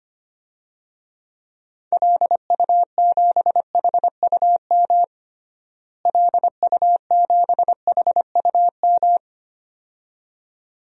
Morse code, heard as a keyed tone, about 25 wpm, 700 Hz. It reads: LU7HUM LU7HUM
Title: DXCC-100 Call Signs - 6 Characters - Repeated 2x - 25wpm